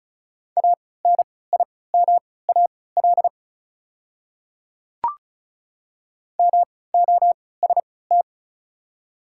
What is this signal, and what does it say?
Morse code, heard as a keyed tone, about 36 wpm, spaced as for 20 wpm, 690 Hz. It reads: ANIMAL MOST
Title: 3x Character Spacing - Top 500 Words - 35wpm